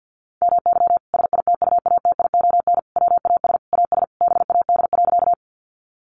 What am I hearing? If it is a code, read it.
Morse code, heard as a keyed tone, about 50 wpm, 700 Hz. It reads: MY SITUATION WAS AS BAD.
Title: Sentences from Top 1200 Words 50wpm